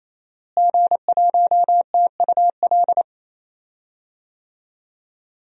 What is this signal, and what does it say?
Morse code, heard as a keyed tone, about 28 wpm, 700 Hz. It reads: G1TUL
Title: DXCC-100 Call Signs - 5 Characters - Repeated 3x - 28wpm